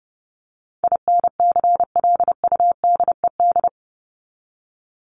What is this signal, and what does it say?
Morse code, heard as a keyed tone, about 30 wpm, 700 Hz. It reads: INCLUDED
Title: ED Suffix - Top 100 Words with Base Words 30wpm